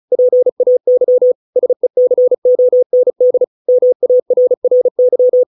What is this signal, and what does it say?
Morse code, heard as a keyed tone, about 35 wpm, 500 Hz. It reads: PAY SECOND MARRY